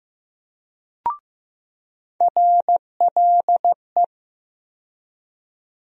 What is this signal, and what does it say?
Morse code, heard as a keyed tone, about 15 wpm, 700 Hz. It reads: RLE
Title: USA Call Sign Postfixes - 3 Letters 15wpm